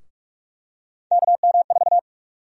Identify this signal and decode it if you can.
Morse code, heard as a keyed tone, about 42 wpm, 700 Hz. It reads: KM4
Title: USA Call Sign Prefixes - Repeated 2x - 45wpm